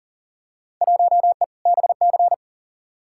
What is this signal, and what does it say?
Morse code, heard as a keyed tone, about 40 wpm, 700 Hz. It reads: 1E BC
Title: Field Day Exchanges - 40wpm